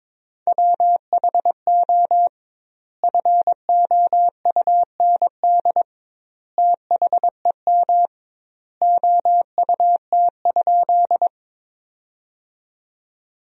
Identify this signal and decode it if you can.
Morse code, heard as a keyed tone, about 22 wpm, 700 Hz. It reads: WHO FOUND THEM OUT?
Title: 2x Word-Spacing - Sentences from Top 300 Words - 22wpm